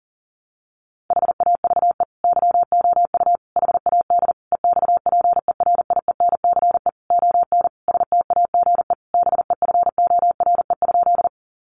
Morse code, heard as a keyed tone, about 40 wpm, 700 Hz. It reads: HAVE YOU HAD EXPERIENCE ON STAGE BEFORE?
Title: Sentences from Top 1200 Words 40wpm